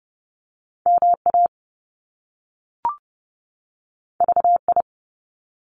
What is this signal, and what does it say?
Morse code, heard as a keyed tone, about 30 wpm, 700 Hz. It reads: MA VI
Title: Top Two Letter Combinations in Words 30wpm